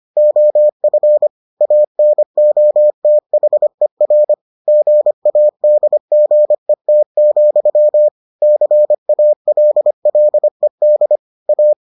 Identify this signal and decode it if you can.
Morse code, heard as a keyed tone, about 25 wpm, 600 Hz. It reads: OF ANOTHER GADGET, CALLED A